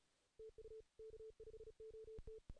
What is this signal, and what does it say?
Morse code, heard as a keyed tone, about 34 wpm, 440 Hz. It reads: TU K5OT
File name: 2016 JAN CW NAQP / 2016 Jan naqp k5ot - 0-2